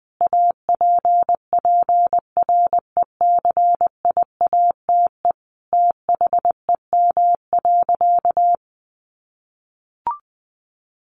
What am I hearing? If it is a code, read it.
Morse code, heard as a keyed tone, about 19 wpm, 700 Hz. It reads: APPRECIATE THEM.
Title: Sentences from Top 1300 Words 20wpm